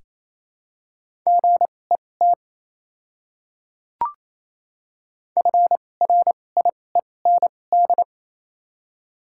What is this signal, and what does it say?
Morse code, heard as a keyed tone, about 28 wpm, spaced as for 20 wpm, 700 Hz. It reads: GET FRIEND
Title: 2x Character Spacing - Top 300 Words - 28wpm